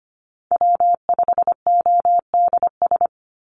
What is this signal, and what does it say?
Morse code, heard as a keyed tone, about 25 wpm, 700 Hz. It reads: W5ODS
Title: NA SOTA Chaser Callsigns 25wpm